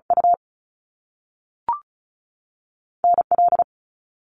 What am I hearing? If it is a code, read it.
Morse code, heard as a keyed tone, about 36 wpm, 700 Hz. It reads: U N L
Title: Canadian Provinces and Territories 35wpm